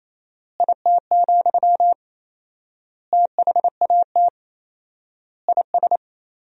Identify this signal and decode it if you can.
Morse code, heard as a keyed tone, about 28 wpm, 700 Hz. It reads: IT, THAT IS
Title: 4x Word-Spacing - Sentences from Top 400 Words - 28wpm